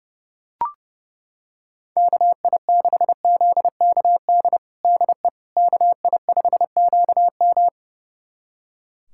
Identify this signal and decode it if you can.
Morse code, heard as a keyed tone, about 32 wpm, 700 Hz. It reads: KI6ZKD DE KI5QM
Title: USA Callsign DE Callsign 30wpm